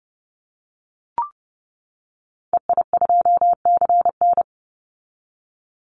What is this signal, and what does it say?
Morse code, heard as a keyed tone, about 30 wpm, 700 Hz. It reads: EI2CN
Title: Call Signs - 30wpm